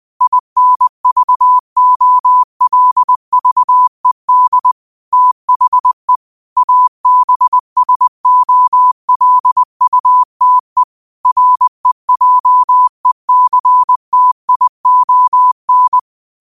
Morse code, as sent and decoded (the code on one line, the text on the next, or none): .. -. ...- --- .-.. ...- . -.. / - .... / .- -... ... --- .-.. ..- - . / .-. .--- . -.-. - .. --- -.
INVOLVED THE ABSOLUTE REJECTION